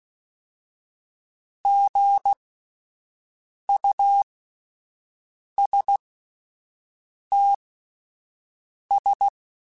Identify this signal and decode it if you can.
Morse code, heard as a keyed tone, about 16 wpm, spaced as for 5 wpm, 800 Hz. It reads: GUSTS
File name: Spaced Characters / Spaced Characters-mp3s / Weather5 5 sp @10